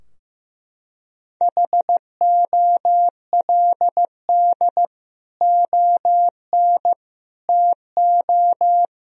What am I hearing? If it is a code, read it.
Morse code, heard as a keyed tone, about 16 wpm, 700 Hz. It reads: HOLD ON TO
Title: Sentences from Top 1000 Words - Review Part 2 - 15wpm